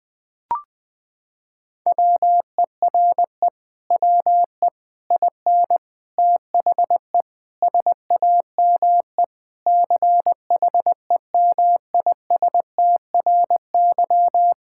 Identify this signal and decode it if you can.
Morse code, heard as a keyed tone, about 21 wpm, 700 Hz. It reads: WERE WE IN THE SAME CHEMISTRY
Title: Sentences from Top 1000 Words - Review Part 6 - 20wpm